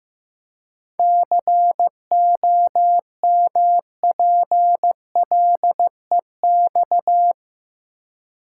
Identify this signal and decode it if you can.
Morse code, heard as a keyed tone, about 15 wpm, 700 Hz. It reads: COMPLEX
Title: Seven Letter Words - 15wpm